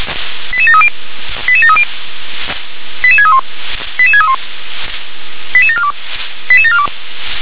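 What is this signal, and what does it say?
Five-tone selective call (ZVEI), heard as quick runs of five tones, tones of about 70 ms. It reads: 88730 88730 88631 88631 88742 88742
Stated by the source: ZVEI five-tone selective call received on an SDR.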